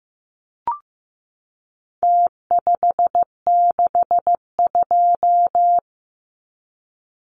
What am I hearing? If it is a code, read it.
Morse code, heard as a keyed tone, about 15 wpm, 700 Hz. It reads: T562
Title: Numbers - Contest Style - 4-Digit 15wpm